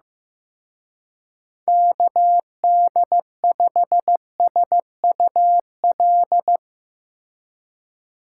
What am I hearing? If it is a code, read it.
Morse code, heard as a keyed tone, about 16 wpm, 700 Hz. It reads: KD5SUL
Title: Speed Racing - USA 2x3 Call Signs 15wpm